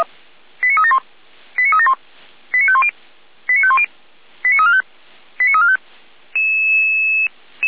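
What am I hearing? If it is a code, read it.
Five-tone selective call (ZVEI), heard as quick runs of five tones, tones of about 70 ms. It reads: 89371 89371 89510 89510 89345 89345
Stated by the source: ZVEI five-tone selective call received via SDR.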